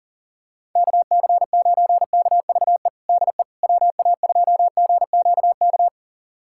Morse code, heard as a keyed tone, about 40 wpm, 700 Hz. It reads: KC9KVE DE WA2GQK
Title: USA Callsign DE Callsign - 6 Characters - 40wpm